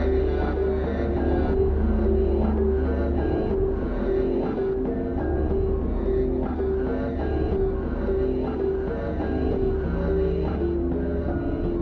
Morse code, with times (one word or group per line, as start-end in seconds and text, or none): none